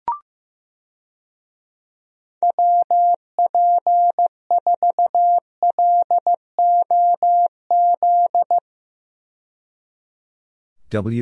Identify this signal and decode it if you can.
Morse code, heard as a keyed tone, about 15 wpm, 700 Hz. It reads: WP4LOZ
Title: USA Call Signs - 6 Characters - 15wpm